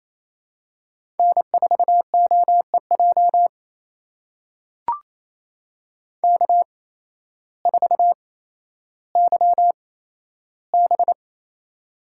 Morse code, heard as a keyed tone, about 28 wpm, spaced as for 13 wpm, 700 Hz. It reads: N4OEJ K4YB
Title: USA Call Signs - 8x Character Spacing - 28wpm